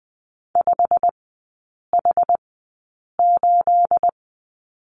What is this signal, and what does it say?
Morse code, heard as a keyed tone, about 20 wpm, spaced as for 8 wpm, 690 Hz. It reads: 5H8
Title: 212 Lesson 34e - Number 8 - Three Characters 20wpm